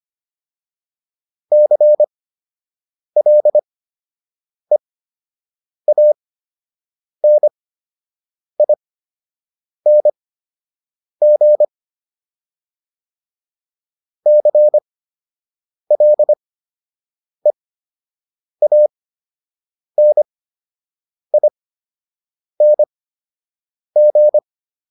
Morse code, heard as a keyed tone, about 25 wpm, spaced as for 7 wpm, 600 Hz. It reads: CLEANING CLEANING